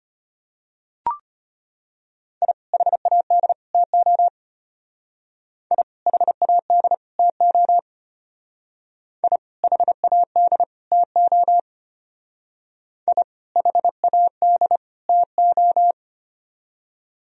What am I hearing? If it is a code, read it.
Morse code, heard as a keyed tone, about 32 wpm, 700 Hz. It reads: I HAD TO I HAD TO I HAD TO I HAD TO